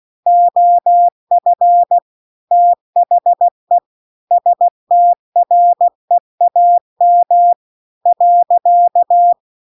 Morse code, heard as a keyed tone, about 16 wpm, 700 Hz. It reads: OF THE STREAM .
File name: CW-The-Big-Twoheaded-River-2-16WPM